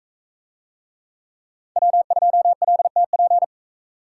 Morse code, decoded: W2LTP

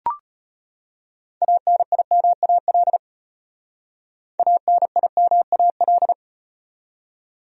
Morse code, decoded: ANIMAL ANIMAL

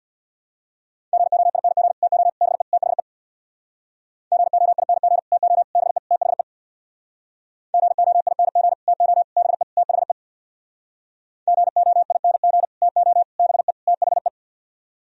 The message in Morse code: GOING TO BE THE GOING TO BE THE GOING TO BE THE GOING TO BE THE

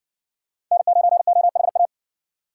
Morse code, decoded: N9OHA

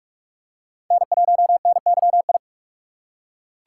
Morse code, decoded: N1NYI

